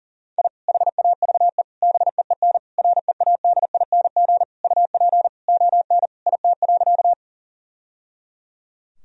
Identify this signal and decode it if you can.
Morse code, heard as a keyed tone, about 40 wpm, 700 Hz. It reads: I HAVE BEEN READING UP ON IT.
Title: Sentences from Top 1000 Words - Review Part 3 - 40wpm